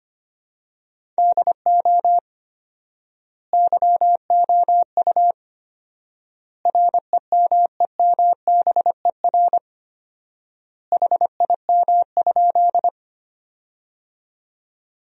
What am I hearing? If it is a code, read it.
Morse code, heard as a keyed tone, about 25 wpm, 700 Hz. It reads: DO YOU REMEMBER HIM?
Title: 4x Word-Spacing - Sentences from Top 400 Words - 25wpm